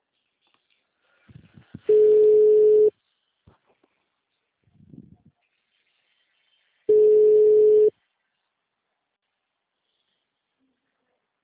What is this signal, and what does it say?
Ringback tone (Europe, 425 Hz).
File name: out- -2028- - - .7846